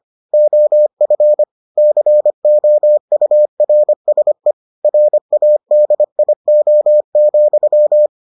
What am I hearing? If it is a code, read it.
Morse code, heard as a keyed tone, about 25 wpm, 600 Hz. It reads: OF COURSE RADIO,